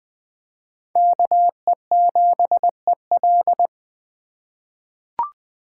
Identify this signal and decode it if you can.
Morse code, heard as a keyed tone, about 21 wpm, 700 Hz. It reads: KE7EL
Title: USA Call Signs - 5 Characters - 4x Character Spacing - 20wpm